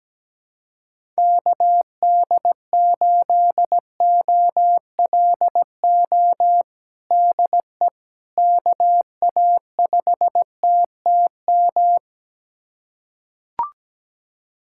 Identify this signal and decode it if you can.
Morse code, heard as a keyed tone, about 17 wpm, 700 Hz. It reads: KD8OLO DE KA5TTM
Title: USA Callsign DE Callsign 17wpm